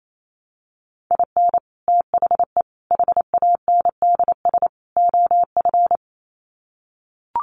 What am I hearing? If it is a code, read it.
Morse code, heard as a keyed tone, about 28 wpm, 700 Hz. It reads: IN THE HANDS OF
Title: Sets of 4 Words 28wpm